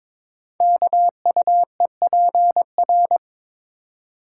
Morse code, heard as a keyed tone, about 22 wpm, 700 Hz. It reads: KUEPR
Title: Code Groups - 5 Letters - 22wpm